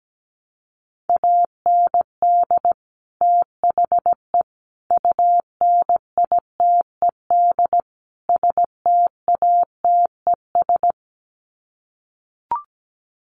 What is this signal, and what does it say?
Morse code, heard as a keyed tone, about 17 wpm, 700 Hz. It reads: AND THE UNITED STATES
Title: Sets of 4 Words 17wpm